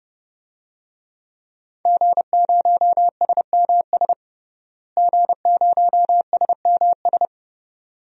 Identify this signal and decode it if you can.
Morse code, heard as a keyed tone, about 30 wpm, 700 Hz. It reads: G0SMS G0SMS
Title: DXCC-100 Call Signs - 5 Characters - Repeated 2x - 30wpm